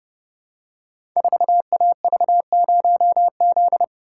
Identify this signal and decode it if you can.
Morse code, heard as a keyed tone, about 30 wpm, 700 Hz. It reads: 4AV0Z
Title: Code Groups - 5 Letter:Numbers - 30wpm